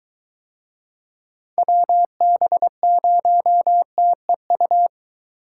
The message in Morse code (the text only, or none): WB0TEU